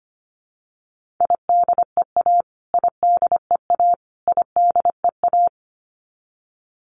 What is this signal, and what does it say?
Morse code, heard as a keyed tone, about 25 wpm, 700 Hz. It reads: IDEA IDEA IDEA
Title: Top 300 Words Repeated 25wpm